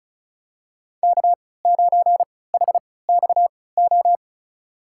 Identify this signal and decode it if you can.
Morse code, heard as a keyed tone, about 36 wpm, spaced as for 20 wpm, 700 Hz. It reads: K9HXO
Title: USA Call Signs - 5 Characters - 3x Character Spacing - 35wpm